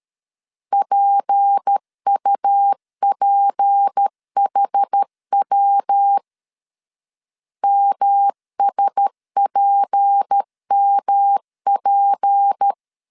Morse code, heard as a keyed tone, about 13 wpm, 800 Hz. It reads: PUPHW MSPMP
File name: LEZIONI / 6 PKX0 / 6 P 12wpm 2033